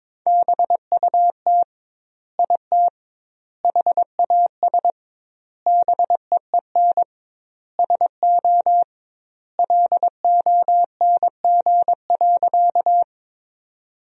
Morse code, heard as a keyed tone, about 22 wpm, 700 Hz. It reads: BUT IT HAS BEEN SO LONG.
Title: 2x Word-Spacing - Sentences from Top 100 Words - 22wpm